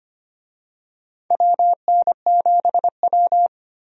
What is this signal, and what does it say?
Morse code, heard as a keyed tone, about 25 wpm, 700 Hz. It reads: WN7W
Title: USA Call Signs - 4 Characters - 25wpm